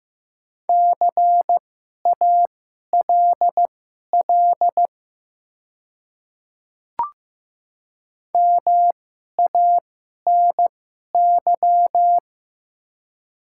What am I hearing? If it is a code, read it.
Morse code, heard as a keyed tone, about 15 wpm, spaced as for 11 wpm, 700 Hz. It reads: CALL MANY
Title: 2x Character Spacing - Top 100 Words - 15wpm